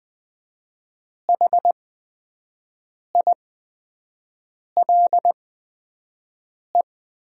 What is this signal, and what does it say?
Morse code, heard as a keyed tone, about 20 wpm, spaced as for 5 wpm, 700 Hz. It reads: HILE
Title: 8x Character Spacing - Top 300 Words - 20wpm